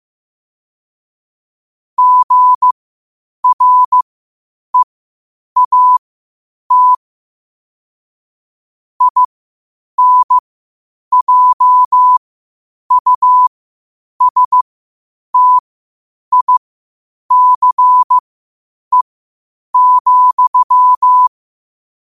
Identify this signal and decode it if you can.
Morse code, heard as a keyed tone, about 15 wpm, spaced as for 8 wpm, 1000 Hz. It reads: GREAT INJUSTICE,
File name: qotdfiles / 8wpm